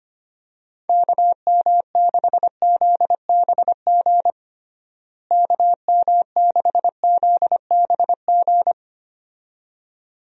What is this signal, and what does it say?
Morse code, heard as a keyed tone, about 25 wpm, 700 Hz. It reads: KM6ZBG KM6ZBG
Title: USA Call Signs - 6 Characters - Repeated 2x - 25wpm